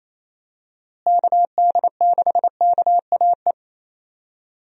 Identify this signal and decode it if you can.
Morse code, heard as a keyed tone, about 28 wpm, 700 Hz. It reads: KD6KAE
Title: USA Call Signs - 6 Characters - Repeated 2x - 28wpm